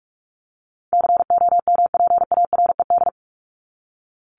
Morse code, heard as a keyed tone, about 44 wpm, 700 Hz. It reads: COMPARED
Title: ED Suffix - Top 100 Words with Base Words 45wpm